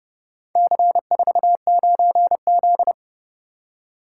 Morse code, heard as a keyed tone, about 29 wpm, 700 Hz. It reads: C49Z